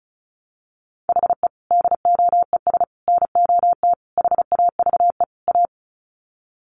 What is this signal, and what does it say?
Morse code, heard as a keyed tone, about 34 wpm, 700 Hz. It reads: HE DOES NOT HAVE A
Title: Sets of 5 Words Encore 35wpm